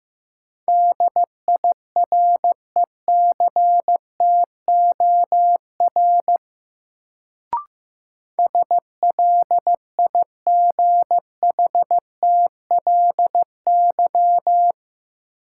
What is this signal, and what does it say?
Morse code, heard as a keyed tone, about 15 wpm, 700 Hz. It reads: DIRECTOR SLIGHTLY
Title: Sentences from Top 1300 Words - New Word Drill - 15wpm